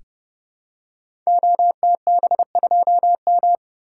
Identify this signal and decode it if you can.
Morse code, heard as a keyed tone, about 30 wpm, 700 Hz. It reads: OTB2M
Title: Code Groups - 5 Letter:Numbers - 30wpm